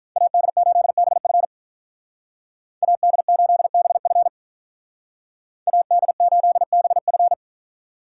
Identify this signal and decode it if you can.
Morse code, heard as a keyed tone, about 44 wpm, 700 Hz. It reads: AD8BF AD8BF AD8BF